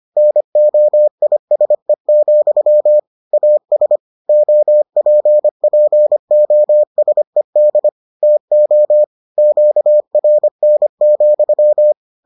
Morse code, heard as a keyed tone, about 25 wpm, 600 Hz. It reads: NOISE, AS OPPOSED TO QRN,